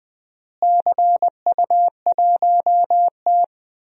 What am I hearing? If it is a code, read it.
Morse code, heard as a keyed tone, about 20 wpm, 700 Hz. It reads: CU1T
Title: DXCC-100 Call Signs - 4 Characters - Repeated 3x - 20wpm